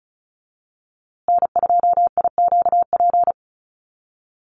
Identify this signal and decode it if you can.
Morse code, heard as a keyed tone, about 34 wpm, 700 Hz. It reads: N2IQP
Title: USA Call Signs Repeated 3x 35wpm